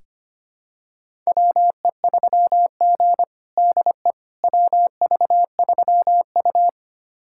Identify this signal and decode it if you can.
Morse code, heard as a keyed tone, about 25 wpm, 700 Hz. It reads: WE3G DE WV3U
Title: USA Callsign DE Callsign - 4 Characters - 25wpm